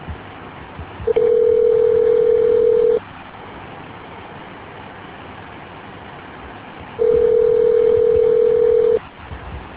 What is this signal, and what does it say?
Ringback tone (North America), 440 + 480 Hz.